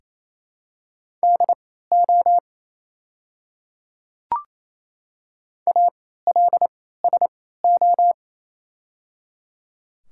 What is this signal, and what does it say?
Morse code, heard as a keyed tone, about 27 wpm, spaced as for 16 wpm, 700 Hz. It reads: DO ALSO
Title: 3x Character Spacing - Top 200 Words - 28wpm